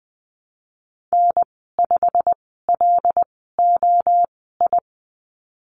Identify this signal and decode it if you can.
Morse code, heard as a keyed tone, about 20 wpm, spaced as for 15 wpm, 700 Hz. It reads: N5LOI
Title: 088 Lesson 15g - Number 5 - Call Signs 20wpm